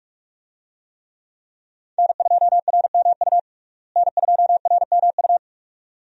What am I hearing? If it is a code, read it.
Morse code, heard as a keyed tone, about 46 wpm, 700 Hz. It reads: N2RMU N2RMU
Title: USA Call Signs - 5 Characters - Repeated 2x - 45wpm